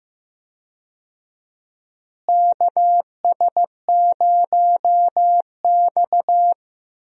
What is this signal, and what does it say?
Morse code, heard as a keyed tone, about 15 wpm, 700 Hz. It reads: KS0X
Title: USA Call Signs - 4 Characters - Repeated 2x - 15wpm